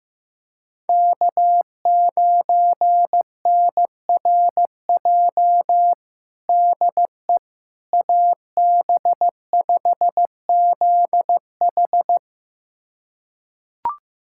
Morse code, heard as a keyed tone, about 15 wpm, 700 Hz. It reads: K9NRJ DE AB5ZH
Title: USA Callsign DE Callsign - 5 Characters - 15wpm